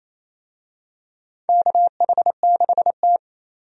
Morse code, heard as a keyed tone, about 28 wpm, 700 Hz. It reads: KH6T
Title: USA Call Signs - 4 Characters - 28wpm